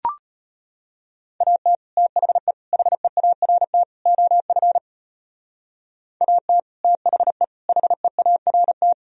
Morse code, heard as a keyed tone, about 33 wpm, 700 Hz. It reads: AT THE HEART OF AT THE HEART